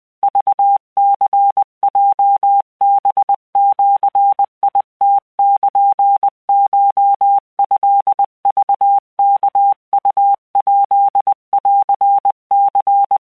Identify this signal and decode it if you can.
Morse code (Japanese Wabun code), heard as a keyed tone, about 20 wpm, 800 Hz. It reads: クニヲハジムルコトクワウヱンニ
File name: keuikuchiyokugo20wptan